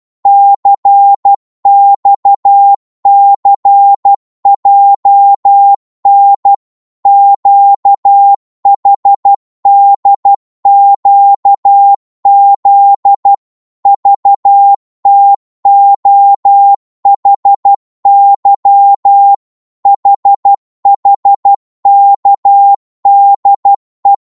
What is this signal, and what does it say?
Morse code, heard as a keyed tone, about 12 wpm, 800 Hz. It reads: CXCJN QHDQZ VTOHY HHKDE
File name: Morse / ltr-12wpm